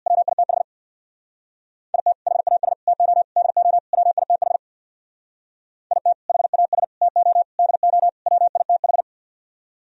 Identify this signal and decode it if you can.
Morse code, heard as a keyed tone, about 65 wpm, 690 Hz. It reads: WITH IT HAS TO DO WITH IT HAS TO DO WITH